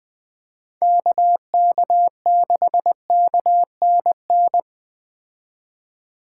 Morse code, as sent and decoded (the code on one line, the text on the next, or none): -.- -.- -.... -.- -. -.
KK6KNN